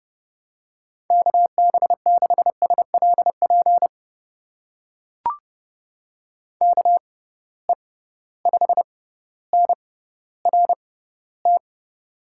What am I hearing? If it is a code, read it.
Morse code, heard as a keyed tone, about 30 wpm, spaced as for 15 wpm, 700 Hz. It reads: KB6SLP KE5NRT
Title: USA Call Signs - 6 Characters - 6x Character Spacing - 30wpm